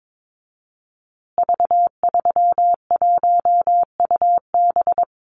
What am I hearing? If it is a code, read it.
Morse code, heard as a keyed tone, about 22 wpm, 700 Hz. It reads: V31UB
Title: Call Signs Repeated 2x - 22wpm